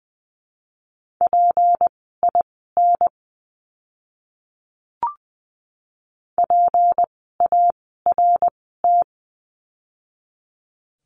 Morse code, heard as a keyed tone, about 20 wpm, spaced as for 15 wpm, 700 Hz. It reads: PIN PART
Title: 153 Lesson 25f - Letter p - Words 20wpm